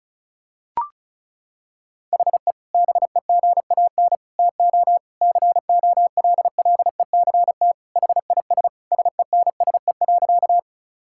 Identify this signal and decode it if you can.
Morse code, heard as a keyed tone, about 36 wpm, 700 Hz. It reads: HE BEGAN TO COLLECT HIS SENSE.
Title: Sentences from Top 1600 Words 35wpm